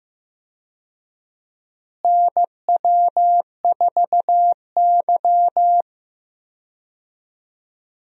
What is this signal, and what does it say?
Morse code, heard as a keyed tone, about 15 wpm, 700 Hz. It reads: NW4Y
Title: USA Call Signs - 4 Characters - Repeated 2x - 15wpm